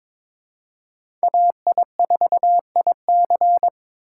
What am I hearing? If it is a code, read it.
Morse code, heard as a keyed tone, about 22 wpm, 700 Hz. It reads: AI4IC
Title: USA Call Signs - 5 Characters - 4x Character Spacing - 22wpm